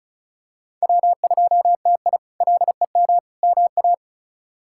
Morse code, heard as a keyed tone, about 36 wpm, 700 Hz. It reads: W2TI LEM MA
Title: K1USN SST QSO Practice - 35wpm